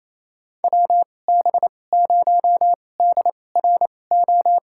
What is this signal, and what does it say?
Morse code, heard as a keyed tone, about 28 wpm, spaced as for 20 wpm, 700 Hz. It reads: WB0DRO
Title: USA Call Signs - 2x Character Spacing - 28wpm